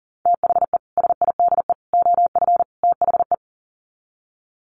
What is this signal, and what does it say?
Morse code, heard as a keyed tone, about 40 wpm, 700 Hz. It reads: THE SIDE OF THE
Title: Sets of 4 Words 40wpm